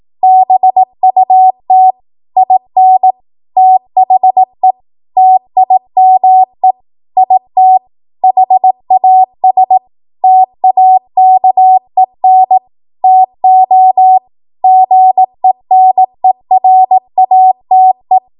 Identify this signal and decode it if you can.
Morse code, heard as a keyed tone, about 18 wpm, 750 Hz. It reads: BUT IN THE TIME IT HAS TAKEN TO GENERATE
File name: Archive / 18 WPM / 18WPM